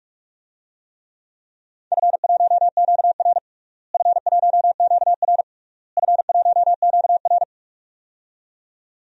Morse code, heard as a keyed tone, about 46 wpm, 700 Hz. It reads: F1QR F1QR F1QR
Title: DXCC-100 Call Signs - 4 Characters - Repeated 3x - 45wpm